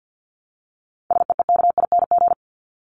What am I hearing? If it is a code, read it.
Morse code, heard as a keyed tone, about 50 wpm, 700 Hz. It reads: SEEKING
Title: ING Suffix - Top 100 Words with Base Words 50wpm